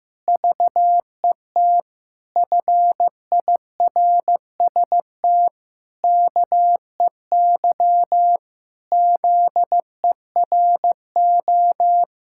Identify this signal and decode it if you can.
Morse code, heard as a keyed tone, about 15 wpm, 700 Hz. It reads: VET FIRST KEY ZERO
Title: Pangram Challenge v2 - 15wpm